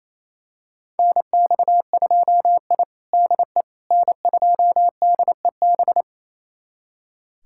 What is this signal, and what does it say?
Morse code, heard as a keyed tone, about 27 wpm, 700 Hz. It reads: NX2I DE N2DEB
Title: USA Callsign DE Callsign 28wpm